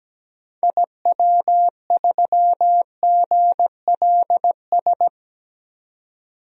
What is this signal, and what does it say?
Morse code, heard as a keyed tone, about 17 wpm, 700 Hz. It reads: IW3GLS
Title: DXCC-100 Call Signs - 6 Characters - 17wpm